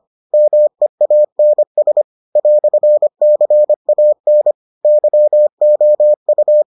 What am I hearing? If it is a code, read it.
Morse code, heard as a keyed tone, about 25 wpm, 600 Hz. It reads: MEANS "CAN YOU